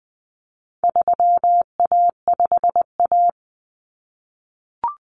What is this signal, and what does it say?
Morse code, heard as a keyed tone, about 20 wpm, 700 Hz. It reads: 3A5A
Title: Common Cut Numbers - 4-Digit 20wpm